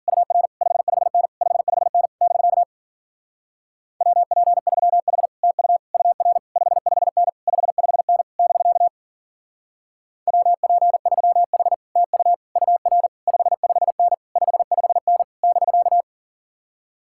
Morse code, heard as a keyed tone, about 50 wpm, 700 Hz. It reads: UR 55N 55N <BK> WP3H TU UR 55N 55N <BK> WP3H TU UR 55N 55N <BK>
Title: Speed-Racing Common QSO Phrases Encore - 40wpm